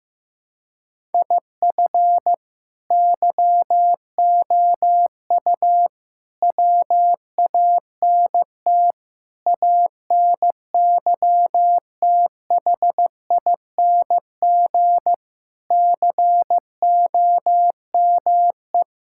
Morse code, heard as a keyed tone, about 15 wpm, 700 Hz. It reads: IF YOU WANT ANYTHING COME
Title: Sentences from Top 1000 Words - Review Part 6 - 15wpm